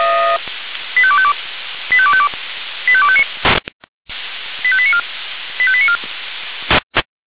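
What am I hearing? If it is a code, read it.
Five-tone selective call (ZVEI), heard as quick runs of five tones, tones of about 70 ms. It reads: 85252 85252 85259 85894 85894
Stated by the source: ZVEI five-tone selective call 85310 received on SDR.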